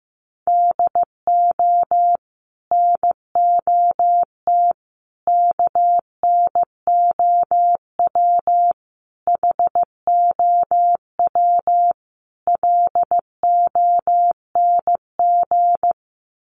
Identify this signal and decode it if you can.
Morse code, heard as a keyed tone, about 15 wpm, 700 Hz. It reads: DO NOT KNOW HOW LONG
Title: Sets of 5 Words Encore2 15wpm